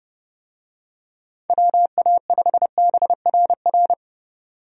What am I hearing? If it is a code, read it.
Morse code, heard as a keyed tone, about 31 wpm, 700 Hz. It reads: WA5BRR
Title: USA Call Signs - 6 Characters - 30wpm